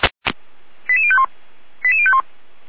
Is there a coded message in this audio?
Yes: five-tone selective call (ZVEI) from 0.9 to 2.2 s, heard as quick runs of five tones, tones of about 70 ms. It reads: 88051 88051